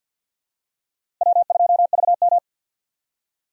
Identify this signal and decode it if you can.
Morse code, heard as a keyed tone, about 46 wpm, 700 Hz. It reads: W2VM
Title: USA Call Signs - 4 Characters - Repeated 3x - 50wpm